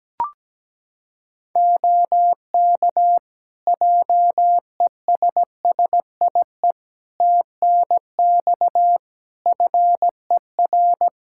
Morse code, heard as a keyed tone, about 17 wpm, 700 Hz. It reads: OK JESSIE TNX FER
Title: Practice QSO - Exchange 4 - Ending - 17wpm